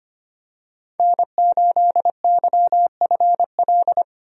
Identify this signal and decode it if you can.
Morse code, heard as a keyed tone, about 25 wpm, 700 Hz. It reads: N8YFL
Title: USA Call Signs - 5 Characters - 25wpm